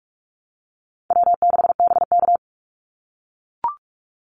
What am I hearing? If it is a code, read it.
Morse code, heard as a keyed tone, about 44 wpm, 700 Hz. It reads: W6BK
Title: Easy Call Signs - 45wpm